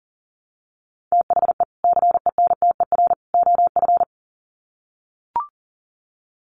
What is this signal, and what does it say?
Morse code, heard as a keyed tone, about 38 wpm, 700 Hz. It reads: THE CENTER OF
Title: Sets of 3 Words 40wpm